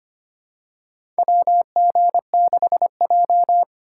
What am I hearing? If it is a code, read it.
Morse code, heard as a keyed tone, about 25 wpm, 700 Hz. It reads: WG6J
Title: USA Call Signs - 4 Characters - 3x Character Spacing - 25wpm